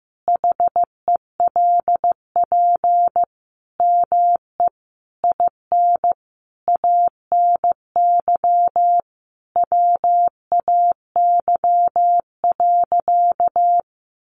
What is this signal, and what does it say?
Morse code, heard as a keyed tone, about 15 wpm, 700 Hz. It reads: HELP ME IN ANY WAY.
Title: Sentences from Top 600 Words 15wpm